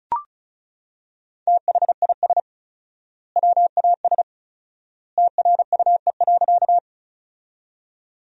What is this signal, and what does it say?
Morse code, heard as a keyed tone, about 34 wpm, 700 Hz. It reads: THIS WAS TRUE.
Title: 4x Word-Spacing - Sentences from Top 400 Words - 35wpm